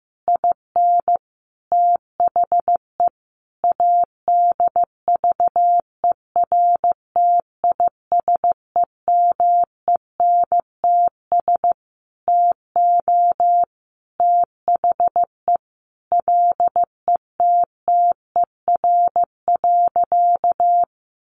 Morse code, heard as a keyed tone, about 15 wpm, 700 Hz. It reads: IN THE ADVERTISEMENTS TO THE LETTER.
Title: A Sound Of Thunder 15wpm